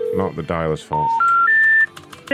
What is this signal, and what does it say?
Special information tone, with speech.